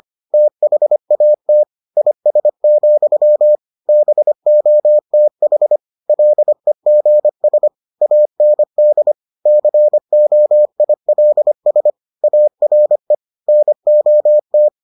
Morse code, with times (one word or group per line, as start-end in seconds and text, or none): 0.3-1.6 THAT
2.0-3.6 IS,
3.9-5.8 BOTH
6.1-7.7 LEGS
8.0-9.1 AND
9.5-11.9 COILS
12.2-13.1 ARE
13.5-14.7 NOT